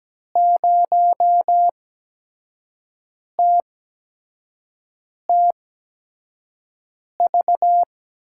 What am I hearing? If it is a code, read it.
Morse code, heard as a keyed tone, about 17 wpm, spaced as for 5 wpm, 700 Hz. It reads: 0TTV
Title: USA Call Signs - 6 Characters - 8x Character Spacing - 17wpm